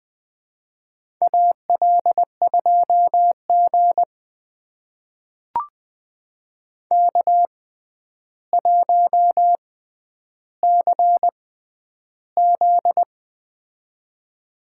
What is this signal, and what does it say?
Morse code, heard as a keyed tone, about 20 wpm, spaced as for 10 wpm, 700 Hz. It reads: AL2G K1CZ
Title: USA Call Signs - 4 Characters - 6x Character Spacing - 20wpm